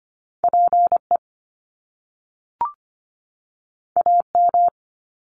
Morse code, heard as a keyed tone, about 25 wpm, 700 Hz. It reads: PE AM
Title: Top Two Letter Combinations in Words 25wpm